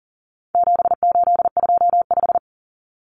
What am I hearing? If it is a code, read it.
Morse code, heard as a keyed tone, about 40 wpm, 700 Hz. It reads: 7825
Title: Numbers - 4-Digit 40wpm